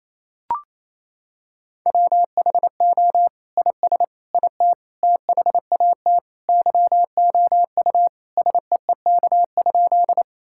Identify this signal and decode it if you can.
Morse code, heard as a keyed tone, about 28 wpm, 700 Hz. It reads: WHO IS IT THAT YOU SEEK?
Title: Sentences from Top 1600 Words 28wpm